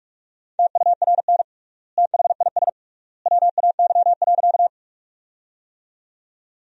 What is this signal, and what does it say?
Morse code, heard as a keyed tone, about 46 wpm, 700 Hz. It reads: TURN THIS WAY.
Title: 3x Word-Spacing - Sentences from Top 200 Words - 45wpm